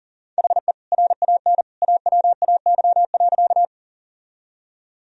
Morse code, heard as a keyed tone, about 38 wpm, 700 Hz. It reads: HE RAN AWAY.
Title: Sentences from Top 1000 Words - Review Part 3 - 40wpm